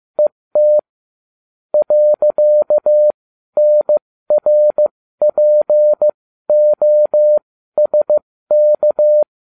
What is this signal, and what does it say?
Morse code, heard as a keyed tone, about 15 wpm, spaced as for 12 wpm, 600 Hz. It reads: ET .NRPOSK